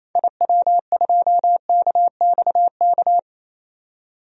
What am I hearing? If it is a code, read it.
Morse code, heard as a keyed tone, about 27 wpm, 700 Hz. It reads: IW2KXK
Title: DXCC-100 Call Signs - 6 Characters - 28wpm